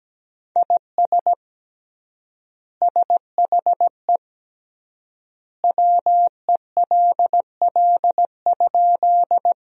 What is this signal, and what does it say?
Morse code, heard as a keyed tone, about 17 wpm, 700 Hz. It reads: IS SHE WELL?
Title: 3x Word-Spacing - Sentences from Top 200 Words - 17wpm